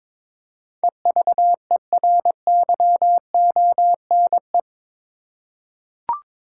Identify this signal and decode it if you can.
Morse code, heard as a keyed tone, about 21 wpm, 700 Hz. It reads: EVERYONE
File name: practice / 22wpm / Eight Letter Words - 22wpm